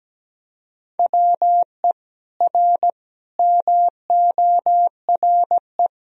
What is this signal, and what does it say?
Morse code, heard as a keyed tone, about 17 wpm, 700 Hz. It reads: WE R MORE
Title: Sentences from Top 500 Words and Abbreviations 17wpm